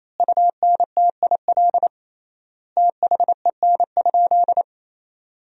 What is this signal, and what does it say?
Morse code, heard as a keyed tone, about 28 wpm, 700 Hz. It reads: UNTIL THEN?
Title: 3x Word-Spacing - Sentences from Top 500 Words - 28wpm